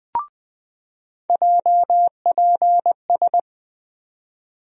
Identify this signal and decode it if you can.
Morse code, heard as a keyed tone, about 20 wpm, 700 Hz. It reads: JPS
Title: USA Call Sign Postfixes - 3 Letters 20wpm